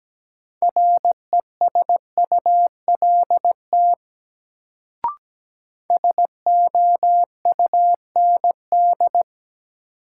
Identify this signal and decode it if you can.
Morse code, heard as a keyed tone, about 17 wpm, 700 Hz. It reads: RESULT SOUND
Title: Sets of 3 Words - New Word Drill - 17wpm